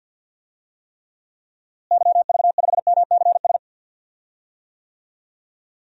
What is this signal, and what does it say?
Morse code, heard as a keyed tone, about 47 wpm, 700 Hz. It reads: YV5MCS
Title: DXCC-100 Call Signs - 6 Characters - Repeated 2x - 50wpm